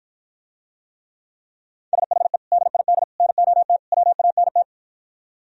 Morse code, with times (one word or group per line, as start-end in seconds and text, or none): 1.9-2.4 SHE
2.5-3.0 DID
3.2-3.8 NOT
3.9-4.6 WANT